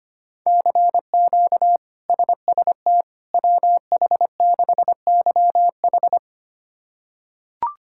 Morse code, heard as a keyed tone, about 25 wpm, 700 Hz. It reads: CQ SST WH6YH
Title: K1USN SST QSO Practice - 25wpm